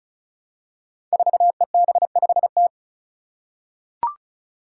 Morse code, heard as a keyed tone, about 36 wpm, 700 Hz. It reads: 4EB5T
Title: Five Challenging Letter-Number Combinations 35wpm